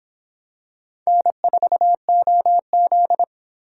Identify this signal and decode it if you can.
Morse code, heard as a keyed tone, about 26 wpm, 700 Hz. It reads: N4OZ